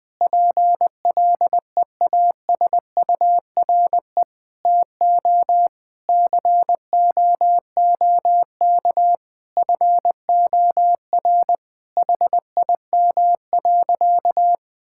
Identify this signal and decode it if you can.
Morse code, heard as a keyed tone, about 20 wpm, 700 Hz. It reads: PLEASURE TO COOK FOR HIM.
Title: Sentences from Top 1600 Words 20wpm